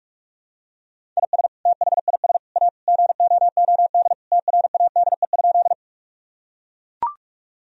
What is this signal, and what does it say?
Morse code, heard as a keyed tone, about 41 wpm, 700 Hz. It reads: IS THIS A GOOD TRADE?
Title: Sentences from Top 1000 Words - Review Part 4 - 45wpm